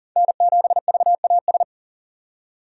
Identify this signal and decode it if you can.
Morse code, heard as a keyed tone, about 40 wpm, 700 Hz. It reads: N7VAS
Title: USA Call Signs - 5 Characters - 6x Character Spacing - 40wpm